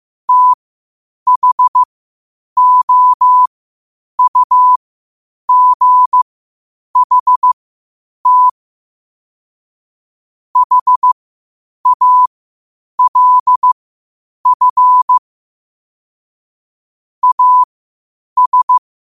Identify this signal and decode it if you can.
Morse code, heard as a keyed tone, about 15 wpm, spaced as for 8 wpm, 1000 Hz. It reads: THOUGHT HALF AS
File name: qotdfiles / 8wpm